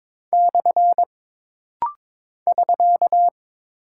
Morse code, heard as a keyed tone, about 22 wpm, 700 Hz. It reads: / <SK>
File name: practice / 22wpm / Single Punctuation-Prosign - Rapid-Fire - 22wpm